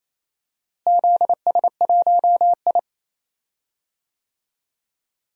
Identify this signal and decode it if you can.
Morse code, heard as a keyed tone, about 28 wpm, 700 Hz. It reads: ZS1I
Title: DXCC-100 Call Signs - 4 Characters - Repeated 2x - 28wpm